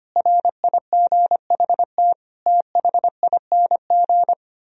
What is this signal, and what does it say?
Morse code, heard as a keyed tone, about 24 wpm, 700 Hz. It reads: RIGHT THING
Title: Speed-Racing - Sets of 5 Words 25wpm